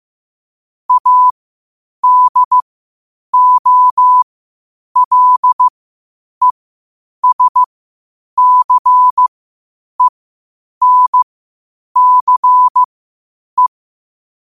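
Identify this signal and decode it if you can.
Morse code, heard as a keyed tone, about 15 wpm, spaced as for 8 wpm, 1000 Hz. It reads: ADOLESCENCE
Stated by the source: Morse practice at 8 WPM from Morse Resource.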